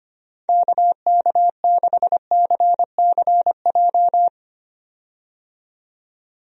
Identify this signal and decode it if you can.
Morse code, heard as a keyed tone, about 25 wpm, 700 Hz. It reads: KK6CCJ
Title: USA Call Signs - 6 Characters - 25wpm